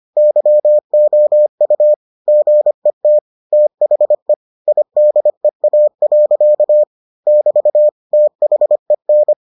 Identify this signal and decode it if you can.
Morse code, heard as a keyed tone, about 25 wpm, 600 Hz. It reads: YOU GET THE IDEA. <BT> THEN